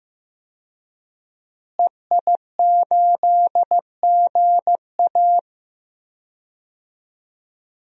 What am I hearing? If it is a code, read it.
Morse code, heard as a keyed tone, about 15 wpm, 700 Hz. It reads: EI8GA
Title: DXCC-100 Call Signs - 5 Characters - Repeated 3x - 15wpm